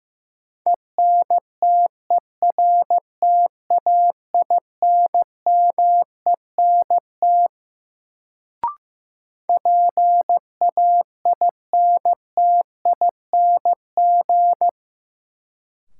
Morse code, heard as a keyed tone, about 15 wpm, 700 Hz. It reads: ENTERTAINMENT PAINTING
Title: Sentences from Top 1000 Words - Review Part 6 - New Word Drill - 15wpm